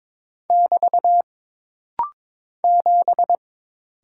Morse code, heard as a keyed tone, about 22 wpm, 700 Hz. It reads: <BT> 7
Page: This rapid-fire timing is geared to push your ICR (Instant Character Recognition) skills.